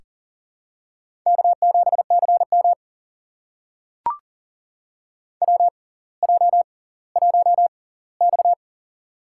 Morse code, heard as a keyed tone, about 38 wpm, spaced as for 20 wpm, 700 Hz. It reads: K7CM WJ1X
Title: USA Call Signs - 4 Characters - 6x Character Spacing - 40wpm